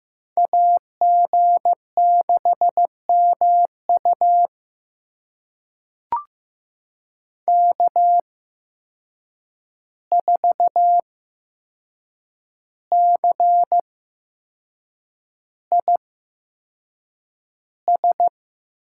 Morse code, heard as a keyed tone, about 15 wpm, spaced as for 6 wpm, 700 Hz. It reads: AG6MU K4CIS
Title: USA Call Signs - 5 Characters - 8x Character Spacing - 15wpm